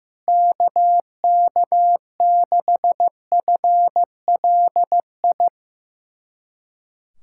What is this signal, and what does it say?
Morse code, heard as a keyed tone, about 15 wpm, 700 Hz. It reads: KK6FLI
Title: USA Callsign DE Callsign - 6 Characters - 15wpm